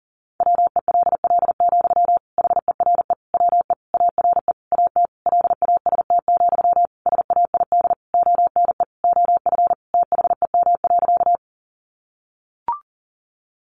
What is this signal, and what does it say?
Morse code, heard as a keyed tone, about 40 wpm, 700 Hz. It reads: WELL, HERE WE ARE AT LAST, SAID ONE OF THEM.
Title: Sentences from Top 300 Words 40wpm